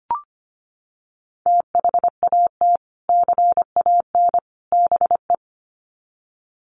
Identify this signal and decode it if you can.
Morse code, heard as a keyed tone, about 25 wpm, 700 Hz. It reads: THAT CAN BE
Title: Sets of 3 Words 25wpm